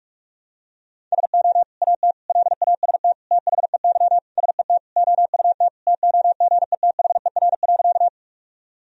Morse code, heard as a keyed tone, about 43 wpm, 700 Hz. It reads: SO AT LAST THEY SET OUT TOGETHER.
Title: Sentences from Top 1000 Words - Review Part 1 - 45wpm